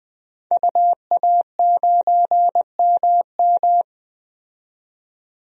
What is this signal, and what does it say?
Morse code, heard as a keyed tone, about 20 wpm, 700 Hz. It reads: UA9MM
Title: DXCC-100 Call Signs - 5 Characters - 20wpm